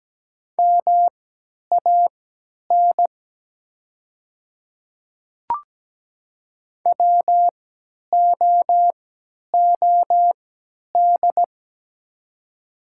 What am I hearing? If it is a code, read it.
Morse code, heard as a keyed tone, about 17 wpm, spaced as for 10 wpm, 700 Hz. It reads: MAN WOOD
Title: 3x Character Spacing - Top 300 Words - 17wpm